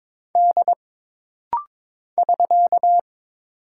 Morse code, heard as a keyed tone, about 22 wpm, 700 Hz. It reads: D <SK>
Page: This rapid-fire timing is geared to push your ICR (Instant Character Recognition) skills.